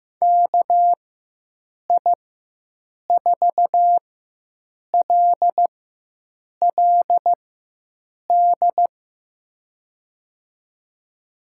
Morse code, heard as a keyed tone, about 15 wpm, spaced as for 7 wpm, 700 Hz. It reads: KI4LLD
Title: USA Call Signs - 4x Character Spacing - 15wpm